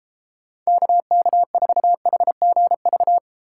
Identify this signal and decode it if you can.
Morse code, heard as a keyed tone, about 34 wpm, 700 Hz. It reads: KK4HGV
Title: Speed Racing - USA 2x3 Call Signs 28wpm